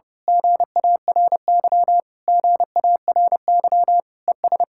Morse code, heard as a keyed tone, about 28 wpm, 700 Hz. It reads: GARY GARY ES
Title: Speed-Racing Common QSO Phrases - 30wpm